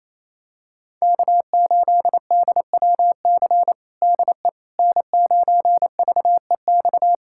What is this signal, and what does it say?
Morse code, heard as a keyed tone, about 28 wpm, 700 Hz. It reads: K8DWC DE N9VEX
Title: USA Callsign DE Callsign - 5 Characters - 28wpm